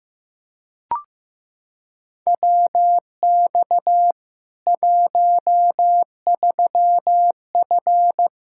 Morse code, heard as a keyed tone, about 15 wpm, 700 Hz. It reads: WX 13F
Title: K1USN SST QSO Practice 15wpm